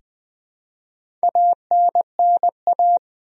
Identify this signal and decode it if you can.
Morse code, heard as a keyed tone, about 20 wpm, 700 Hz. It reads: ANNA
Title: USA Common Names - Max 5 Letters - 20wpm